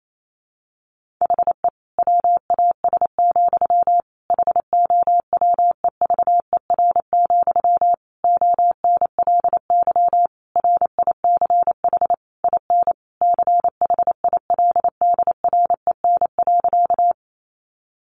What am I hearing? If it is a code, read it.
Morse code, heard as a keyed tone, about 28 wpm, 700 Hz. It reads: HE WAS, HOWEVER, ONLY RICH IN CHILDREN.
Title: Sentences from Top 1100 Words 28wpm